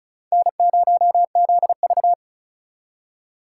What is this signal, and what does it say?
Morse code, heard as a keyed tone, about 48 wpm, spaced as for 42 wpm, 700 Hz. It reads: N0ZV
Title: Speed Racing - USA 1x2 Call Signs 35wpm